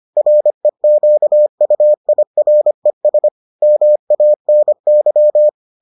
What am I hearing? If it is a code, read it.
Morse code, heard as a keyed tone, about 25 wpm, 600 Hz. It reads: REQUIRES MANY